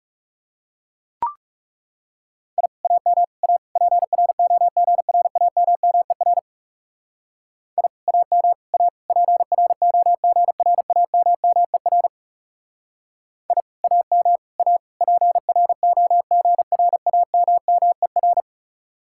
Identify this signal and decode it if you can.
Morse code, heard as a keyed tone, about 37 wpm, 700 Hz. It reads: I AM A PROGRAMMER I AM A PROGRAMMER I AM A PROGRAMMER
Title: Speed-Racing Common QSO Phrases - 30wpm